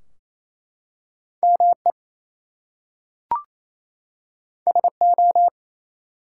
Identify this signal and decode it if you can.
Morse code, heard as a keyed tone, about 29 wpm, 700 Hz. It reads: ME SO
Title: Two Letter Words - 28wpm